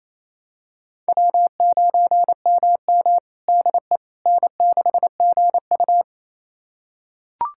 Morse code, heard as a keyed tone, about 28 wpm, 700 Hz. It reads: W9MM DE N6GU
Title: USA Callsign DE Callsign - 4 Characters - 28wpm